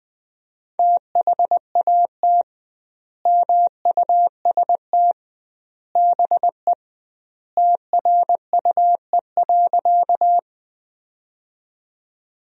Morse code, heard as a keyed tone, about 21 wpm, 700 Hz. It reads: THAT MUST BE TRUE.
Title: 2x Word-Spacing - Sentences from Top 400 Words - 20wpm